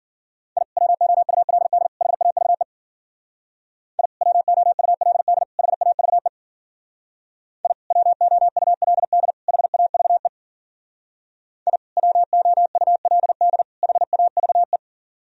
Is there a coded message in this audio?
Yes: Morse code from 0.6 to 14.8 s, heard as a keyed tone, about 50 wpm, 700 Hz. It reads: I WOULD HAVE I WOULD HAVE I WOULD HAVE I WOULD HAVE